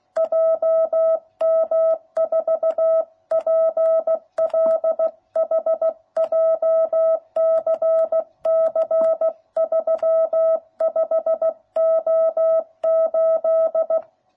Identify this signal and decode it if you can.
Morse code, heard as a keyed tone, about 16 wpm, 660 Hz. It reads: JM4PLH JCC 35O8